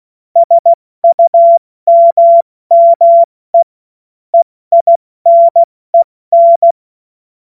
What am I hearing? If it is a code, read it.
Morse code, heard as a keyed tone, about 16 wpm, 680 Hz. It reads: SUMME EINEN
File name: Morsekurs / Witze / Witz 06. September 2022 70 F80